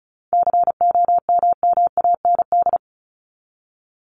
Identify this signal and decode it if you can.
Morse code, heard as a keyed tone, about 34 wpm, 700 Hz. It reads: COMMAND
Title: Top 1501-1750 Words 35wpm